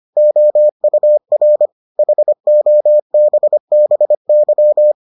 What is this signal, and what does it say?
Morse code, heard as a keyed tone, about 25 wpm, 600 Hz. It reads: OUR HOBBY